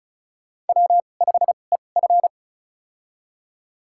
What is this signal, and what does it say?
Morse code, heard as a keyed tone, about 34 wpm, spaced as for 25 wpm, 700 Hz. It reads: W5EF
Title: USA Call Signs - 4 Characters - 2x Character Spacing - 35wpm